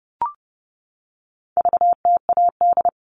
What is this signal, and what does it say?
Morse code, heard as a keyed tone, about 30 wpm, 700 Hz. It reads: VTAD
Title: All Cut Numbers - 4-Digit 30wpm